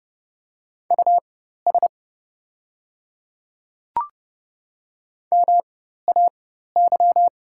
Morse code, heard as a keyed tone, about 30 wpm, spaced as for 14 wpm, 700 Hz. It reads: US MAY
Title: 4x Character Spacing - Top 200 Words - 30wpm